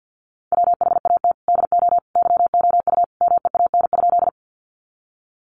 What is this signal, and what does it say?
Morse code, heard as a keyed tone, about 50 wpm, 700 Hz. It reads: WHAT DO YOU MEAN?
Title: Sentences from Top 200 Words 50wpm